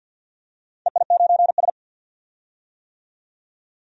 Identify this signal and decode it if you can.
Morse code, heard as a keyed tone, about 47 wpm, 700 Hz. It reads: EI9S
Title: DXCC-100 Call Signs - 4 Characters - 50wpm